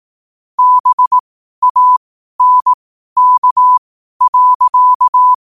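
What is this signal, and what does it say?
Morse code, heard as a keyed tone, about 18 wpm, spaced as for 12 wpm, 1000 Hz. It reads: BANK.
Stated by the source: Morse code sent at 12 words per minute.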